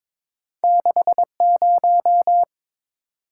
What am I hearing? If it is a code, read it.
Morse code, heard as a keyed tone, about 22 wpm, 700 Hz. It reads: 60
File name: practice / 22wpm / Two Letter-Number Combinations - 22wpm